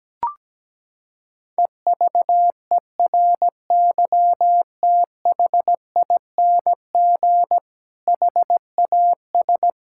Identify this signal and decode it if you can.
Morse code, heard as a keyed tone, about 17 wpm, 700 Hz. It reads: EVERYTHING HAS
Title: Sentences from Top 1600 Words 17wpm